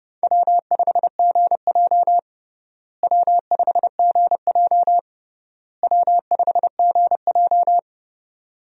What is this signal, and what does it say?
Morse code, heard as a keyed tone, about 30 wpm, 700 Hz. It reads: W5GJ W5GJ W5GJ
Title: USA Call Signs - 4 Characters - Repeated 3x - 30wpm